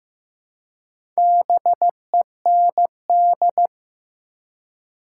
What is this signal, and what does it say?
Morse code, heard as a keyed tone, about 15 wpm, 700 Hz. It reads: BEND
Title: Four Letter Words - 15wpm